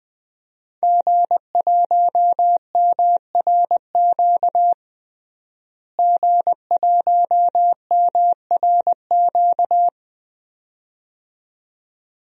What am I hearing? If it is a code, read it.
Morse code, heard as a keyed tone, about 20 wpm, 700 Hz. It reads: G1MRQ G1MRQ
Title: DXCC-100 Call Signs - 5 Characters - Repeated 2x - 20wpm